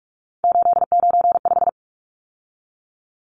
Morse code, heard as a keyed tone, about 44 wpm, 700 Hz. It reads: 895
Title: Numbers - 3-Digit 45wpm